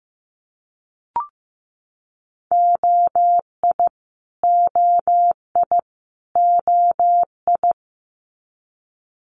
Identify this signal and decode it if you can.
Morse code, heard as a keyed tone, about 15 wpm, 700 Hz. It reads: OI OI OI
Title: Multi-letter Phonemes Repeated 3x 15wpm